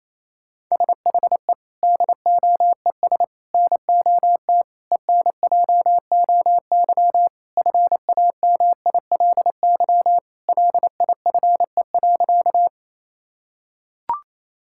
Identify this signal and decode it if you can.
Morse code, heard as a keyed tone, about 28 wpm, 700 Hz. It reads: SHE DOES NOT ENJOY FAMILY LIFE.
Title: Sentences from Top 1000 Words - Review Part 5 - 28wpm